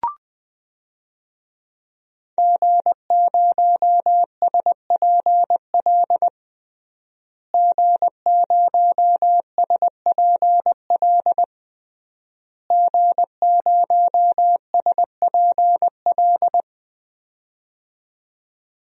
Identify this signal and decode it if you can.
Morse code, heard as a keyed tone, about 20 wpm, 700 Hz. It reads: G0SPL G0SPL G0SPL
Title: DXCC-100 Call Signs - 5 Characters - Repeated 3x - 20wpm